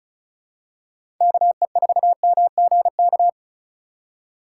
Morse code, heard as a keyed tone, about 36 wpm, 700 Hz. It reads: KE4MGK
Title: USA Call Signs - 4x Character Spacing - 35wpm